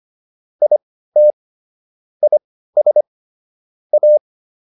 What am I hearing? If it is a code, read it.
Morse code, heard as a keyed tone, about 25 wpm, spaced as for 15 wpm, 600 Hz. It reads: IT IS A